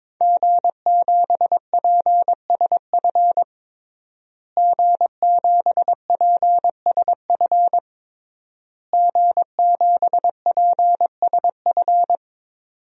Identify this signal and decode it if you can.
Morse code, heard as a keyed tone, about 22 wpm, 700 Hz. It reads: G7PSF G7PSF G7PSF
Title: DXCC-100 Call Signs - 5 Characters - Repeated 3x - 22wpm